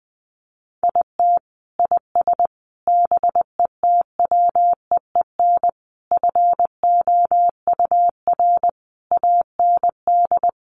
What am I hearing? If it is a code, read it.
Morse code, heard as a keyed tone, about 20 wpm, 700 Hz. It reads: IT IS BETWEEN FOUR AND
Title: Sentences from Top 1500 Words 20wpm